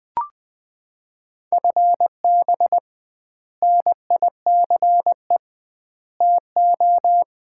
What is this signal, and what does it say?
Morse code, heard as a keyed tone, about 21 wpm, 700 Hz. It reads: FB NICE TO
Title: Common QSO Phrases - 2x Word-Spacing - 20wpm